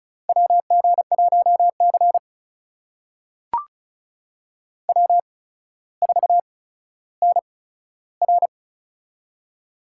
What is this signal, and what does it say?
Morse code, heard as a keyed tone, about 36 wpm, spaced as for 15 wpm, 700 Hz. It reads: WG1C W4NR
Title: USA Call Signs - 4 Characters - 8x Character Spacing - 35wpm